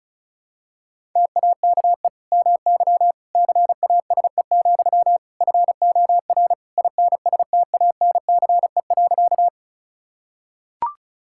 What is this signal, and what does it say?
Morse code, heard as a keyed tone, about 36 wpm, 700 Hz. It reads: TAKE MY CASE, FOR INSTANCE.
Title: Sentences from Top 1000 Words - Review Part 5 - 35wpm